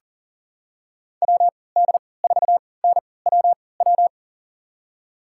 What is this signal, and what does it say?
Morse code, heard as a keyed tone, about 39 wpm, spaced as for 22 wpm, 700 Hz. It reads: WD4NWW
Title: USA Call Signs - 6 Characters - 3x Character Spacing - 40wpm